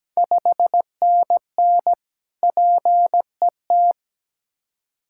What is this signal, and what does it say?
Morse code, heard as a keyed tone, about 17 wpm, 700 Hz. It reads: 5NN PET
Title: Ontario QSO Party Exchange 17wpm